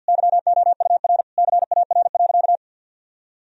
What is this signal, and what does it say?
Morse code, heard as a keyed tone, about 48 wpm, 700 Hz. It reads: YOUR CAR.
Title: Sentences from Top 1000 Words - Review Part 1 - 50wpm